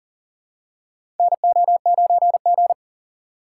Morse code, heard as a keyed tone, about 40 wpm, 700 Hz. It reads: NO9G
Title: USA Call Signs - 4x Character Spacing - 40wpm